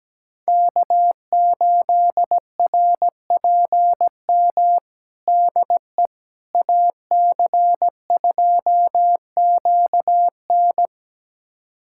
Morse code, heard as a keyed tone, about 17 wpm, 700 Hz. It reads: K8RPM DE AC2QN
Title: USA Callsign DE Callsign - 5 Characters - 17wpm